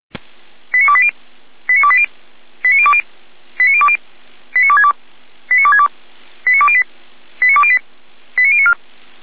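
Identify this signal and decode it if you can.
Five-tone selective call (ZVEI), heard as quick runs of five tones, tones of about 70 ms. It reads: 89280 89280 89020 89020 89262 89262 89208 89208 89094
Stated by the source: Five-tone ZVEI selective call received with an SDR, from the audio server of the Feuerwehr Orlamünde fire brigade.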